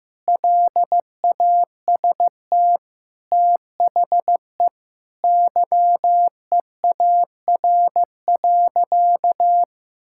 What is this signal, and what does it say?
Morse code, heard as a keyed tone, about 15 wpm, 700 Hz. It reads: LAST THE YEAR.
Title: Sentences from Top 1600 Words 15wpm